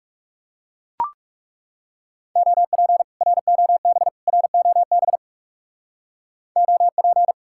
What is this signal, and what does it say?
Morse code, heard as a keyed tone, about 35 wpm, 700 Hz. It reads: OP ROB ROB OP
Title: Speed-Racing Common QSO Phrases Encore - 30wpm